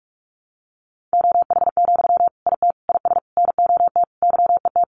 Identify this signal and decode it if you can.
Morse code, heard as a keyed tone, about 44 wpm, 700 Hz. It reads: OH, IT IS NOT YET
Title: Sentences from Top 1100 Words 45wpm